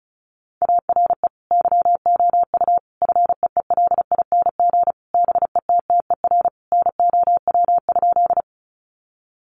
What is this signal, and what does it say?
Morse code, heard as a keyed tone, about 34 wpm, 700 Hz. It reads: ARE YOU FEELING BETTER NOW?
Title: Sentences from Top 1200 Words 35wpm